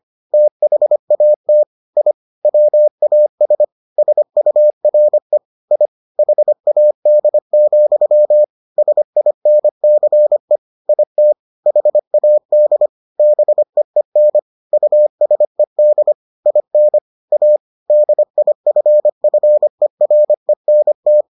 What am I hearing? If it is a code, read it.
Morse code, heard as a keyed tone, about 25 wpm, 600 Hz. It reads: THAT I WAS SURE I HAD, SINCE IT HAD BEEN USED IN A DIFFERENT